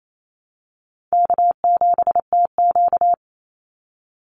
Morse code, heard as a keyed tone, about 28 wpm, 700 Hz. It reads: K7TQ
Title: NA SOTA Chaser Callsigns 28wpm